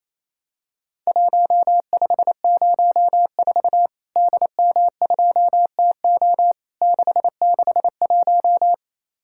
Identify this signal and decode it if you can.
Morse code, heard as a keyed tone, about 28 wpm, 700 Hz. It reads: 1504 DM2TO 661
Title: Worked All Europe DX Contest - QTC Exchange - 28wpm